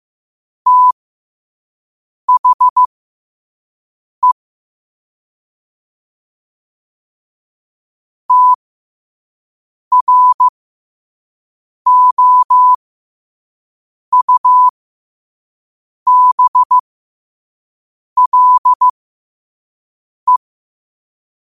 Morse code, heard as a keyed tone, about 15 wpm, spaced as for 5 wpm, 1000 Hz. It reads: THE TROUBLE
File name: qotdfiles / 5wpm